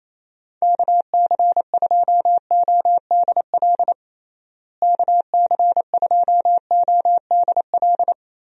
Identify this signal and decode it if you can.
Morse code, heard as a keyed tone, about 27 wpm, 700 Hz. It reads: KC2ODL KC2ODL
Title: USA Call Signs - 6 Characters - Repeated 2x - 28wpm